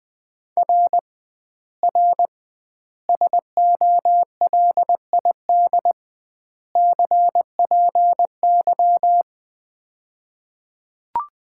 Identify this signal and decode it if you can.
Morse code, heard as a keyed tone, about 20 wpm, 700 Hz. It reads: R R SOLID CPY
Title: Common QSO Phrases - 2x Word-Spacing - 20wpm